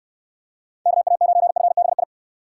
Morse code, heard as a keyed tone, about 62 wpm, 700 Hz. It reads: KA9FBI